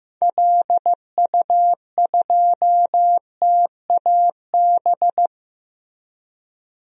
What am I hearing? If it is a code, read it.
Morse code, heard as a keyed tone, about 15 wpm, 700 Hz. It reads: LU2TAB
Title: DXCC-100 Call Signs - 6 Characters - Repeated 2x - 15wpm